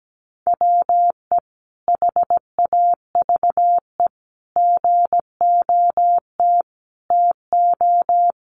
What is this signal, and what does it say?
Morse code, heard as a keyed tone, about 17 wpm, 700 Hz. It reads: WE HAVE GOT TO
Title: Sentences from Top 1200 Words 17wpm